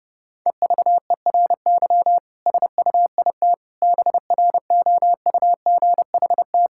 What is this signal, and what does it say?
Morse code, heard as a keyed tone, about 30 wpm, 700 Hz. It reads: EVERY SUIT BROUGHT
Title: Sentences from Top 1600 Words 30wpm